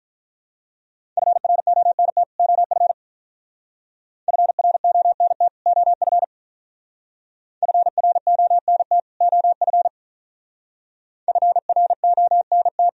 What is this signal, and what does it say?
Morse code, heard as a keyed tone, about 44 wpm, 700 Hz. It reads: FRONT OF FRONT OF FRONT OF FRONT
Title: Speed-Racing - Sets of 2 Words 35wpm